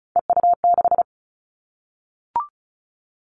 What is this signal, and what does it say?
Morse code, heard as a keyed tone, about 36 wpm, 700 Hz. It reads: EU6
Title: Callsign Prefixes for Top 50 DXCC - 35wpm